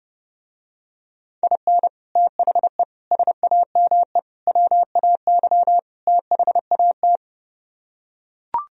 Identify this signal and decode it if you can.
Morse code, heard as a keyed tone, about 32 wpm, 700 Hz. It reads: IN THE SAME WAY THAT